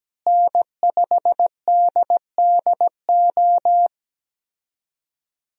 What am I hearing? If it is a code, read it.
Morse code, heard as a keyed tone, about 17 wpm, 700 Hz. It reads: N5DDO
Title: USA Call Signs - 5 Characters - Repeated 3x - 17wpm